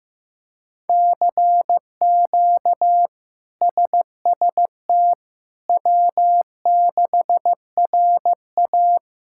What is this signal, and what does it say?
Morse code, heard as a keyed tone, about 15 wpm, 700 Hz. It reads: CQ SST W6RA
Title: K1USN SST QSO Practice - 15wpm